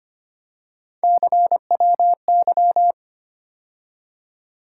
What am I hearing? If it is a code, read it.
Morse code, heard as a keyed tone, about 25 wpm, 700 Hz. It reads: CWY
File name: Three Letter-Number Combinations - 25wpm